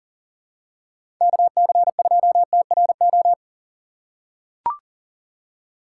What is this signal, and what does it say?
Morse code, heard as a keyed tone, about 39 wpm, 700 Hz. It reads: KC2TRO
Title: USA Call Signs - 6 Characters - 6x Character Spacing - 40wpm